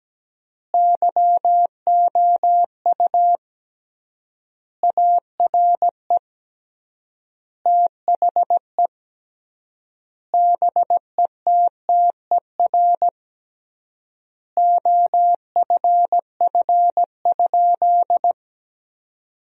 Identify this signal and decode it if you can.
Morse code, heard as a keyed tone, about 17 wpm, 700 Hz. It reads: YOU ARE THE BETTER OFF?
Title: 3x Word-Spacing - Sentences from Top 500 Words - 17wpm